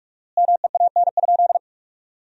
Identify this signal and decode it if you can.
Morse code, heard as a keyed tone, about 46 wpm, 700 Hz. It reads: MEAN?
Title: 4x Word-Spacing - Sentences from Top 200 Words - 45wpm